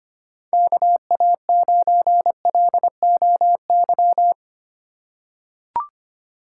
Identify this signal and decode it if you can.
Morse code, heard as a keyed tone, about 25 wpm, 700 Hz. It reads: KA9LOY
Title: USA Call Signs - 6 Characters - 3x Character Spacing - 25wpm